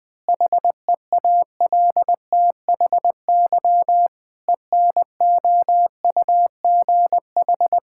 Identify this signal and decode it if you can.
Morse code, heard as a keyed tone, about 20 wpm, 700 Hz. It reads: HEALTHY ENOUGH
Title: Sentences from Top 1000 Words - Review Part 5 - 20wpm